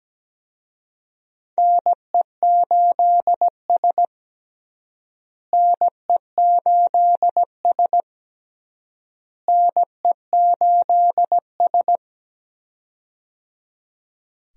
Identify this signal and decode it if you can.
Morse code, heard as a keyed tone, about 17 wpm, 700 Hz. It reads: NE8S NE8S NE8S
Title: USA Call Signs - 4 Characters - Repeated 3x - 17wpm